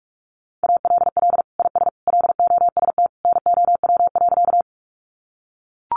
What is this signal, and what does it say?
Morse code, heard as a keyed tone, about 44 wpm, 700 Hz. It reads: ALL IS LOST NOW.
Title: Sentences from Top 1400 Words 45wpm